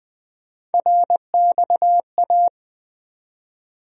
Morse code, heard as a keyed tone, about 20 wpm, 700 Hz. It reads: RXA
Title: USA Call Sign Postfixes - 3 Letters 20wpm